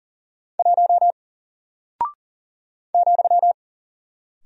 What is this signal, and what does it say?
Morse code, heard as a keyed tone, about 40 wpm, 700 Hz. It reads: 1 ,
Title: Single Character (Letter:Number:Punctuation:Prosign) Alt 40wpm